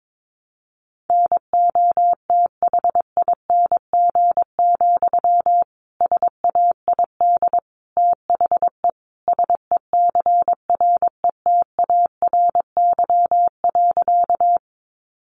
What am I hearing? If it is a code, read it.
Morse code, heard as a keyed tone, about 22 wpm, 700 Hz. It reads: NOTHING, SAID THE SECRETARY.
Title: Sentences from Top 1300 Words 22wpm